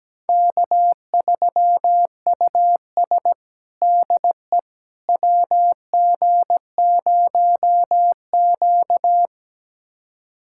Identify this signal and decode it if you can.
Morse code, heard as a keyed tone, about 17 wpm, 700 Hz. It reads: K3US DE WG0Q
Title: USA Callsign DE Callsign - 4 Characters - 17wpm